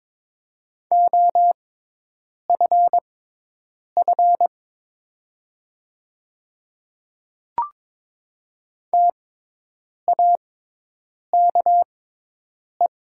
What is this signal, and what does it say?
Morse code, heard as a keyed tone, about 22 wpm, spaced as for 8 wpm, 700 Hz. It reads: OFF TAKE
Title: 6x Character Spacing - Top 200 Words - 22wpm